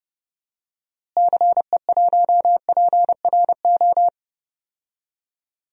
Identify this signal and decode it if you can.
Morse code, heard as a keyed tone, about 30 wpm, 700 Hz. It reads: CE1PRO
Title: DXCC-100 Call Signs - 6 Characters - Repeated 2x - 30wpm